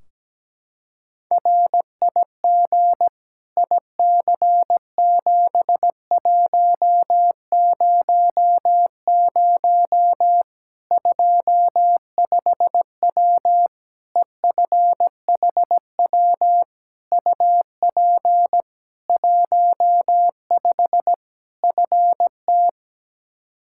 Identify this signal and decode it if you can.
Morse code, heard as a keyed tone, about 17 wpm, 700 Hz. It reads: RIG IC7100 25W EFHW UP 15 FT
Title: Practice QSO - Exchange 2 - Info - 17wpm